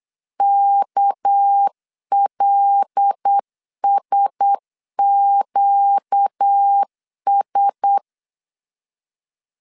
Morse code, heard as a keyed tone, about 8 wpm, 800 Hz. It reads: KLSQS
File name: 8wpm